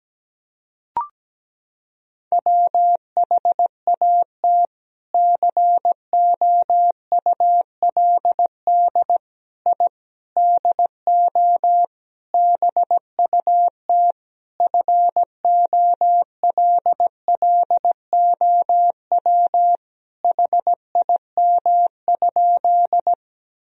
Morse code, heard as a keyed tone, about 17 wpm, 700 Hz. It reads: WHAT COULD I DO BUT FOLLOW HIM?
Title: Sentences from Top 1000 Words - Review Part 1 - 17wpm